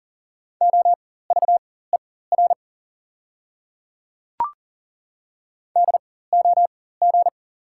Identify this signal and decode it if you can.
Morse code, heard as a keyed tone, about 40 wpm, spaced as for 19 wpm, 690 Hz. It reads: OVER DOG
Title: 4x Character Spacing - Top 500 Words - 40wpm